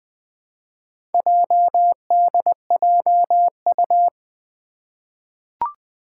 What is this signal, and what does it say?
Morse code, heard as a keyed tone, about 20 wpm, 700 Hz. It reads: JDJU
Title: Four Challenging Letter-Number Combinations 20wpm